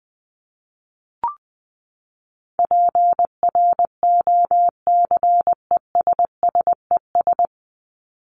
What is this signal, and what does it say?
Morse code, heard as a keyed tone, about 19 wpm, 700 Hz. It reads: PROCESSES
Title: ES:S Suffix - Top 100 Words with Base Words 20wpm